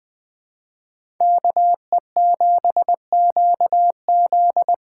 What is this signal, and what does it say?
Morse code, heard as a keyed tone, about 20 wpm, 700 Hz. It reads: KE7QZ
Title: USA Call Signs - 6 Characters - 6x Character Spacing - 20wpm